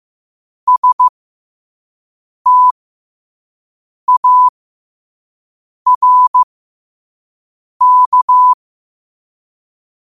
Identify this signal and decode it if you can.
Morse code, heard as a keyed tone, about 15 wpm, spaced as for 5 wpm, 1000 Hz. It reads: STARK
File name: qotdfiles / 5wpm